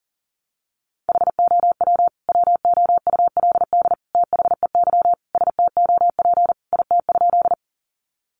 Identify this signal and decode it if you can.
Morse code, heard as a keyed tone, about 40 wpm, 700 Hz. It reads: HOW WOULD THEY STOP IT?
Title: Sentences from Top 300 Words 40wpm